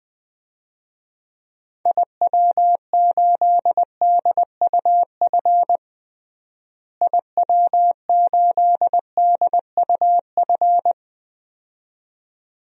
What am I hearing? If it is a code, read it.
Morse code, heard as a keyed tone, about 20 wpm, 700 Hz. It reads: IW8DUF IW8DUF
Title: DXCC-100 Call Signs - 6 Characters - Repeated 2x - 20wpm